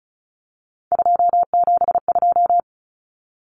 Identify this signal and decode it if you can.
Morse code, heard as a keyed tone, about 34 wpm, 700 Hz. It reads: 272
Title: Numbers - 3-Digit 35wpm